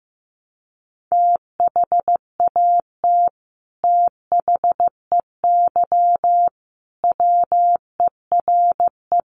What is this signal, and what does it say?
Morse code, heard as a keyed tone, about 15 wpm, 700 Hz. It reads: THAT THEY WERE
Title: Sets of 5 Words Encore 15wpm